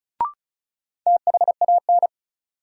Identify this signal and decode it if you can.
Morse code, heard as a keyed tone, about 34 wpm, 700 Hz. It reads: THAN
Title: Sentences from Top 100 Words - New Word Drill - 35wpm